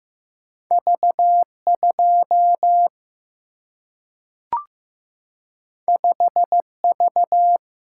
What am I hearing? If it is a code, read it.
Morse code, heard as a keyed tone, about 15 wpm, 700 Hz. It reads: V2 5V
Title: Single-Letter Single-Number 15wpm